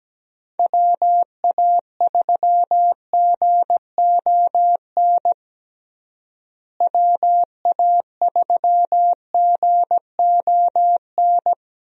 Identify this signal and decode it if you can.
Morse code, heard as a keyed tone, about 17 wpm, 700 Hz. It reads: WA3GON WA3GON
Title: USA Call Signs - 6 Characters - Repeated 2x - 17wpm